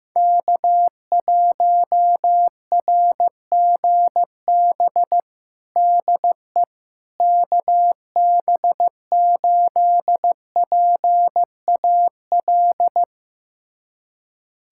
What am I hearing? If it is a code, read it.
Morse code, heard as a keyed tone, about 15 wpm, 700 Hz. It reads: K1RGB DE KB8PAL
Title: USA Callsign DE Callsign 15wpm